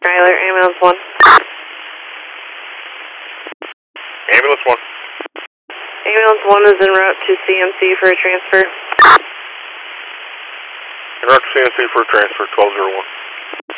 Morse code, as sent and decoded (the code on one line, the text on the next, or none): none